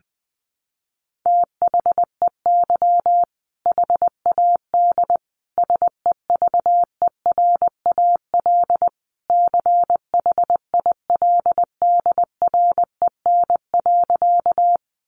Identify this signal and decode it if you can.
Morse code, heard as a keyed tone, about 20 wpm, 700 Hz. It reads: THEY HAD SEVERAL CHILDREN.
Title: Sentences from Top 400 Words 20wpm